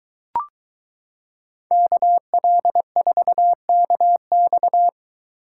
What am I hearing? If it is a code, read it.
Morse code, heard as a keyed tone, about 23 wpm, 700 Hz. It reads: KL4KX